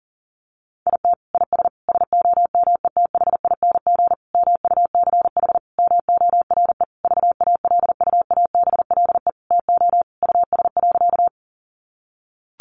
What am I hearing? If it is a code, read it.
Morse code, heard as a keyed tone, about 40 wpm, 700 Hz. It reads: IT IS SOMETHING MUCH MORE VALUABLE TO US.
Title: Sentences from Top 1100 Words 40wpm